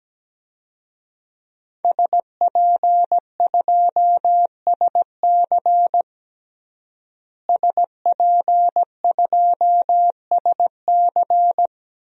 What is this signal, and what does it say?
Morse code, heard as a keyed tone, about 17 wpm, 700 Hz. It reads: SP2SC SP2SC
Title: DXCC-100 Call Signs - 5 Characters - Repeated 3x - 17wpm